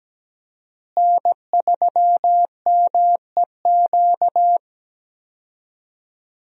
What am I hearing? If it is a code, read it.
Morse code, heard as a keyed tone, about 17 wpm, 700 Hz. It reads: N3MEQ
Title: USA Call Signs - 5 Characters - Repeated 2x - 17wpm